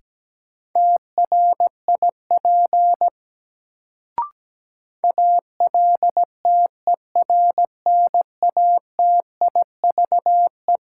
Morse code, heard as a keyed tone, about 17 wpm, 700 Hz. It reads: TRIP ALTERNATIVE
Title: Sentences from Top 1000 Words - Review Part 4 - New Word Drill - 17wpm